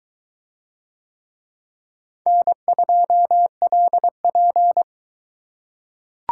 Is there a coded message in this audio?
Yes: Morse code from 2.3 to 4.8 s, heard as a keyed tone, about 23 wpm, 700 Hz. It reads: N2LP